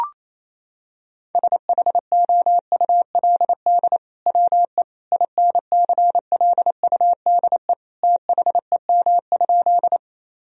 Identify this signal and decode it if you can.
Morse code, heard as a keyed tone, about 27 wpm, 700 Hz. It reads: SHOULD WE INCLUDE THEM?
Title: Sentences from Top 1000 Words - Review Part 3 - 28wpm